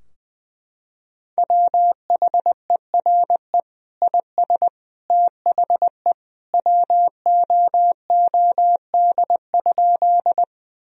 Morse code, heard as a keyed tone, about 20 wpm, 700 Hz. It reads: WHERE IS THE WOOD?
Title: Sentences from Top 1000 Words - Review Part 2 - 20wpm